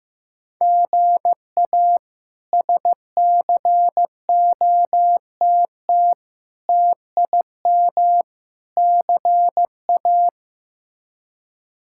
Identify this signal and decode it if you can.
Morse code, heard as a keyed tone, about 15 wpm, 700 Hz. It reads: GA SCOTT TIM CA
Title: K1USN SST QSO Practice - 15wpm